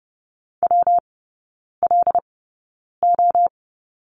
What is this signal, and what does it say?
Morse code, heard as a keyed tone, about 30 wpm, spaced as for 9 wpm, 700 Hz. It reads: WLO
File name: Lesson 19e - Letter w - Three Characters 30wpm